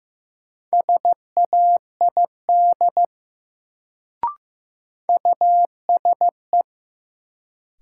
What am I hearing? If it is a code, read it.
Morse code, heard as a keyed tone, about 15 wpm, 700 Hz. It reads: SAID USE
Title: Sentences from Top 100 Words - New Word Drill - 15wpm